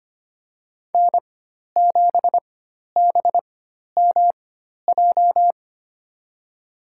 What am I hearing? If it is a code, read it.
Morse code, heard as a keyed tone, about 25 wpm, spaced as for 12 wpm, 700 Hz. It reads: N7BMJ
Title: USA Call Signs - 5 Characters - 4x Character Spacing - 25wpm